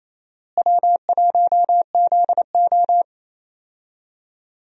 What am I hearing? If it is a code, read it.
Morse code, heard as a keyed tone, about 28 wpm, 700 Hz. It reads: W1ZO
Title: USA Call Signs - 4 Characters - Repeated 2x - 28wpm